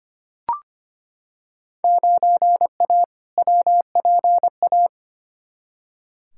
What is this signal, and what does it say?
Morse code, heard as a keyed tone, about 25 wpm, 700 Hz. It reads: 9A WPA
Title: Field Day Exchanges Encore - 25wpm